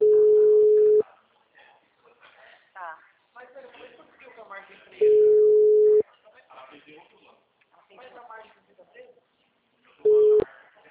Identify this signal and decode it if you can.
Ringback tone (Europe, 425 Hz), with speech.
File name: out- -2003- - - .686